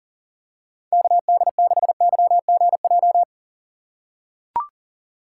Morse code, heard as a keyed tone, about 40 wpm, 700 Hz. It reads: KD6YGJ
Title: USA Call Signs - 6 Characters - 2x Character Spacing - 40wpm